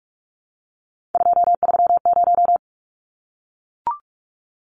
Morse code, heard as a keyed tone, about 44 wpm, 700 Hz. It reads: 230
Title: Numbers - 3-Digit 45wpm